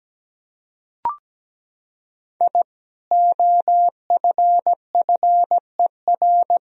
Morse code, heard as a keyed tone, about 17 wpm, 700 Hz. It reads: I OFFER
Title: Sentences from Top 1000 Words - Review Part 4 - 17wpm